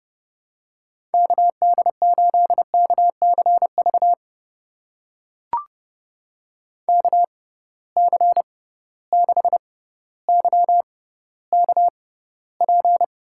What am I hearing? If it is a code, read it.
Morse code, heard as a keyed tone, about 30 wpm, spaced as for 15 wpm, 700 Hz. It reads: KD8KCV KC6YKP
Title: USA Call Signs - 6 Characters - 6x Character Spacing - 30wpm